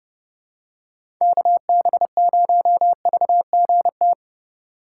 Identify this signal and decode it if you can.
Morse code, heard as a keyed tone, about 30 wpm, 700 Hz. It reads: KB0VGT
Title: USA Call Signs - 6 Characters - 3x Character Spacing - 30wpm